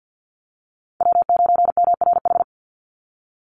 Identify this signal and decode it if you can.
Morse code, heard as a keyed tone, about 50 wpm, 700 Hz. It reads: W9MRH
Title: NA SOTA Chaser Callsigns 50wpm